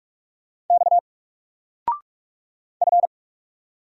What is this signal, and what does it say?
Morse code, heard as a keyed tone, about 46 wpm, 700 Hz. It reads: X F
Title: Single Character (Letter:Number:Punctuation:Prosign) 45wpm